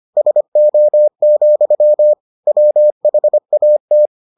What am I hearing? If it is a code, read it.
Morse code, heard as a keyed tone, about 25 wpm, 600 Hz. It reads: SO, WHAT